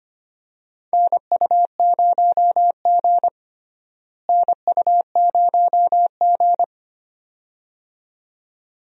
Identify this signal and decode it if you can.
Morse code, heard as a keyed tone, about 25 wpm, 700 Hz. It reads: NU0G NU0G
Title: USA Call Signs - 4 Characters - Repeated 2x - 25wpm